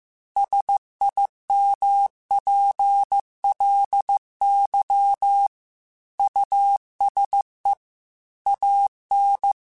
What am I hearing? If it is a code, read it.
Morse code, heard as a keyed tone, about 15 wpm, 800 Hz. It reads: SIMPLY USE AN
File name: Technical Subjects / Technical Subjects-mp3s / Sw Pwr Sup @13